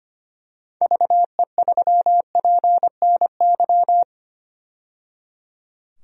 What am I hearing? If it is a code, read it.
Morse code, heard as a keyed tone, about 25 wpm, 700 Hz. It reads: VE3PNY
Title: DXCC-100 Call Signs - 6 Characters - 25wpm